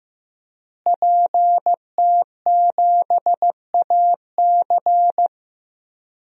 Morse code, heard as a keyed tone, about 15 wpm, 700 Hz. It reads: PT7AC
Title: DXCC-100 Call Signs - 5 Characters - Repeated 3x - 15wpm